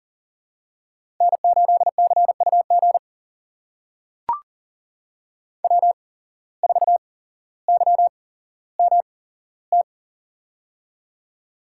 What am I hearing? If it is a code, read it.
Morse code, heard as a keyed tone, about 40 wpm, spaced as for 17 wpm, 700 Hz. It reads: N8CUG W4YMT
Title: USA Call Signs - 5 Characters - 8x Character Spacing - 40wpm